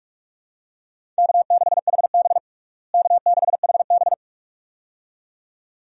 Morse code, heard as a keyed tone, about 46 wpm, 700 Hz. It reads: K6HB K6HB
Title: USA Call Signs - 4 Characters - Repeated 2x - 45wpm